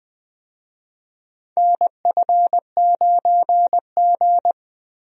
Morse code, heard as a keyed tone, about 20 wpm, 700 Hz. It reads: NF9G
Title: USA Call Signs - 4 Characters - Repeated 3x - 20wpm